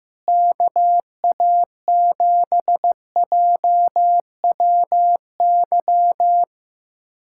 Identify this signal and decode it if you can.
Morse code, heard as a keyed tone, about 15 wpm, 700 Hz. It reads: KA7JWY
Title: USA Callsign DE Callsign - 6 Characters - 15wpm